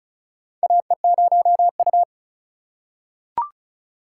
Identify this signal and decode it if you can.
Morse code, heard as a keyed tone, about 36 wpm, 700 Hz. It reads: AE0U
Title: USA Call Signs - 4 Characters - 2x Character Spacing - 35wpm